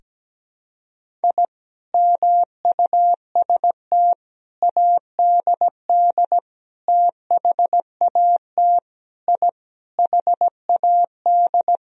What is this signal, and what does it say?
Morse code, heard as a keyed tone, about 17 wpm, 700 Hz. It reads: I MUST ADD THAT I HAD
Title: Sentences from Top 1000 Words - Review Part 1 - 17wpm